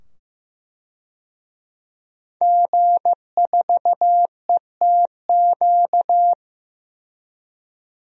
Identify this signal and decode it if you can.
Morse code, heard as a keyed tone, about 15 wpm, 700 Hz. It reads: G4ETQ
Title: DXCC-100 Call Signs - 5 Characters - 15wpm